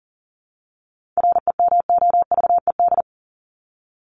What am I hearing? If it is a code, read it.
Morse code, heard as a keyed tone, about 40 wpm, 700 Hz. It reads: REMOVED
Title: ED Suffix - Top 100 Words with Base Words 40wpm